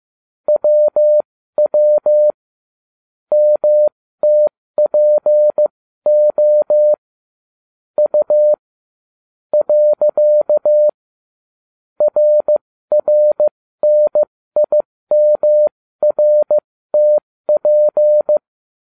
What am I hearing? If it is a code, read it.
Morse code, heard as a keyed tone, about 15 wpm, spaced as for 13 wpm, 600 Hz. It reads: WW MTPO U . RRNIMRTP